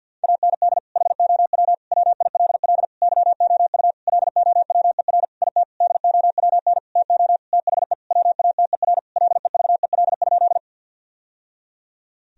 Morse code, heard as a keyed tone, about 50 wpm, 700 Hz. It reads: AND HOW WILL YOU LOWER IT DOWN TO THE WATER LEVEL?
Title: Sentences from Top 1000 Words - Review Part 4 - 50wpm